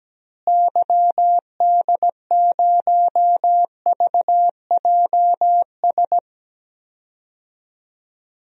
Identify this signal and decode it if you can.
Morse code, heard as a keyed tone, about 17 wpm, 700 Hz. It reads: YD0VJS
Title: DXCC-100 Call Signs - 6 Characters - Repeated 2x - 17wpm